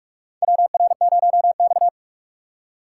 Morse code, heard as a keyed tone, about 46 wpm, 700 Hz. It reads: WR0X
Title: USA Call Signs - 4 Characters - 2x Character Spacing - 45wpm